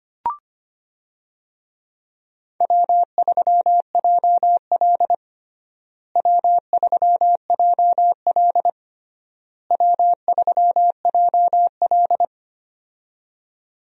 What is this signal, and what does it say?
Morse code, heard as a keyed tone, about 25 wpm, 700 Hz. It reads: W3JL W3JL W3JL
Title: USA Call Signs - 4 Characters - Repeated 3x - 25wpm